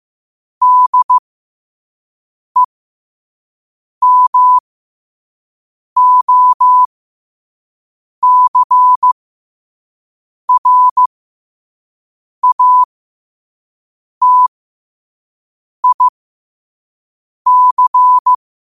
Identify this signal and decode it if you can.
Morse code, heard as a keyed tone, about 15 wpm, spaced as for 5 wpm, 1000 Hz. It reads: DEMOCRATIC
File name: qotdfiles / 5wpm